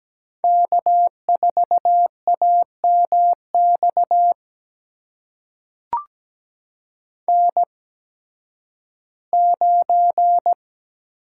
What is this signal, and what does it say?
Morse code, heard as a keyed tone, about 17 wpm, spaced as for 11 wpm, 700 Hz. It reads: K4AMX N9
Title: USA Call Signs - 5 Characters - 8x Character Spacing - 17wpm